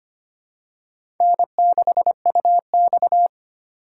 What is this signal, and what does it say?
Morse code, heard as a keyed tone, about 25 wpm, 700 Hz. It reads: N6UX
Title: USA Call Signs - 4 Characters - 3x Character Spacing - 25wpm